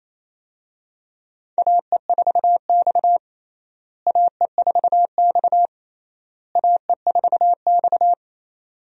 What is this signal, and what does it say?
Morse code, heard as a keyed tone, about 28 wpm, 700 Hz. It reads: AE4X AE4X AE4X
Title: USA Call Signs - 4 Characters - Repeated 3x - 28wpm